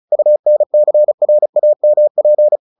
Morse code, heard as a keyed tone, about 34 wpm, 600 Hz. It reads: UNCRAMP
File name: the old man and the sea / the old man and the sea 1 35wpm